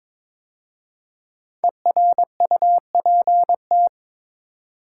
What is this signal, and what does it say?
Morse code, heard as a keyed tone, about 22 wpm, 700 Hz. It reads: ERUPT
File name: 5 Letter Words - First 3 Letters Given - 22wpm